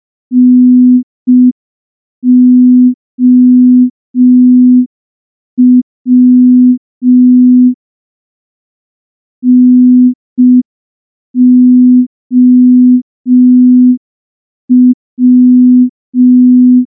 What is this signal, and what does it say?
Morse code, heard as a keyed tone, about 5 wpm, 250 Hz. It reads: NOW NOW
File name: morse / morse-training-05WPM